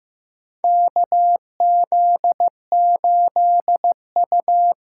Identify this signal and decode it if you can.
Morse code, heard as a keyed tone, about 16 wpm, 700 Hz. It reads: KZ8U
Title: USA Call Signs - 4 Characters - 4x Character Spacing - 15wpm